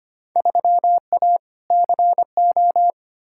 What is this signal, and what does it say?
Morse code, heard as a keyed tone, about 25 wpm, 700 Hz. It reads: 3A CO
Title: Field Day Exchanges Encore - 25wpm